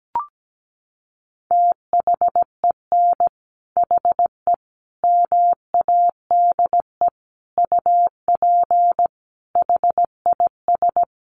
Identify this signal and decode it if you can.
Morse code, heard as a keyed tone, about 17 wpm, 700 Hz. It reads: THEN HE MADE UP HIS
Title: Sentences from Top 1400 Words 17wpm